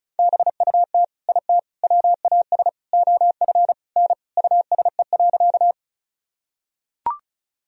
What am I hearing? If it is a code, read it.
Morse code, heard as a keyed tone, about 34 wpm, 700 Hz. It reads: BUT IT WAS OF N USE.
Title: Sentences from Top 100 Words and Abbreviations 35wpm